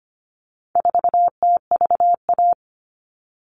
Morse code, heard as a keyed tone, about 25 wpm, 700 Hz. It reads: 4TVA
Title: All Cut Numbers - 4-Digit 25wpm